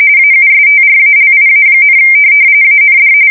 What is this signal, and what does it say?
RTTY, about 45.45 baud. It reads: AUSTRALIA, AND
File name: fsk170 / USS-Missouri